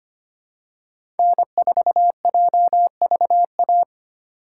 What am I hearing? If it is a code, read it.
Morse code, heard as a keyed tone, about 25 wpm, 700 Hz. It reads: N4JVA
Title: USA Call Signs - 5 Characters - 4x Character Spacing - 25wpm